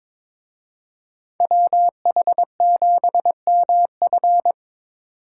Morse code, heard as a keyed tone, about 26 wpm, 700 Hz. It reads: WH7MF
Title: Speed Racing - USA 2x2 Call Signs 22wpm